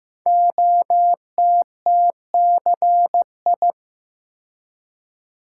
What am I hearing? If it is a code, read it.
Morse code, heard as a keyed tone, about 15 wpm, 700 Hz. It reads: OTTCI
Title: Code Groups - 5 Letters - 15wpm